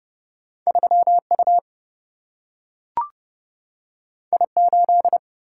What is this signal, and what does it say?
Morse code, heard as a keyed tone, about 30 wpm, 700 Hz. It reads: 3 U I 8
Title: Single-Letter Single-Number 30wpm